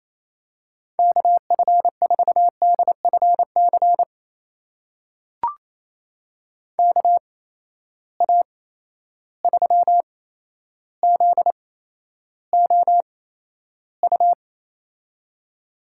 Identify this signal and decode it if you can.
Morse code, heard as a keyed tone, about 27 wpm, spaced as for 12 wpm, 700 Hz. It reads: KF4DFC KA3ZOU
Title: USA Call Signs - 6 Characters - 8x Character Spacing - 28wpm